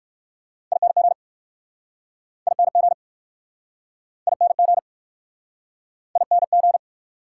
Morse code, heard as a keyed tone, about 60 wpm, 690 Hz. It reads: ING ING ING ING